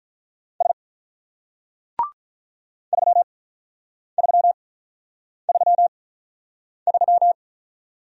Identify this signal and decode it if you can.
Morse code, heard as a keyed tone, about 46 wpm, spaced as for 9 wpm, 700 Hz. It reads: S 3333